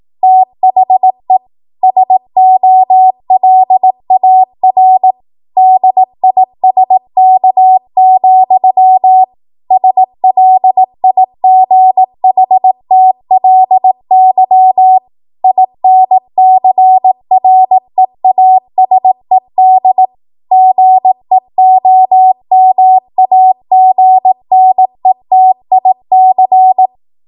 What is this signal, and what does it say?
Morse code, heard as a keyed tone, about 18 wpm, 750 Hz. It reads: THE SOLAR DISK, SLIGHTLY INCREASED GEOMAGNETIC
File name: Morse / 18WPM